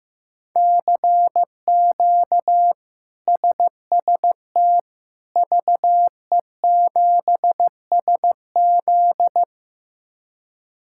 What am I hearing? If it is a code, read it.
Morse code, heard as a keyed tone, about 16 wpm, 700 Hz. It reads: CQ SST VE7SZ
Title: K1USN SST QSO Practice - 15wpm